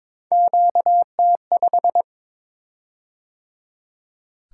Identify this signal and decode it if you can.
Morse code, heard as a keyed tone, about 22 wpm, 700 Hz. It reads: QT5
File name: practice / 22wpm / Three Letter-Number Combinations - 22wpm